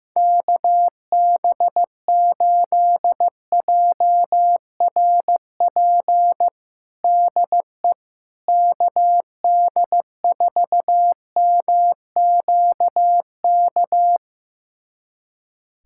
Morse code, heard as a keyed tone, about 15 wpm, 700 Hz. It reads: KB8JRP DE KD4MQK
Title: USA Callsign DE Callsign - 6 Characters - 15wpm